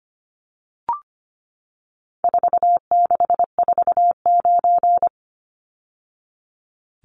Morse code, heard as a keyed tone, about 25 wpm, 700 Hz. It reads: 4649
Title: Numbers - 4-Digit 25wpm